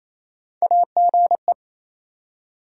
Morse code, heard as a keyed tone, about 27 wpm, 700 Hz. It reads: AGE
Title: Three Letter Words - 28wpm